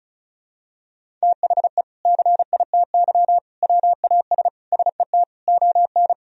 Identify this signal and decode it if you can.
Morse code, heard as a keyed tone, about 34 wpm, 700 Hz. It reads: THE CITY WAS SET ON
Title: Sentences from Top 1000 Words - Review Part 1 - 35wpm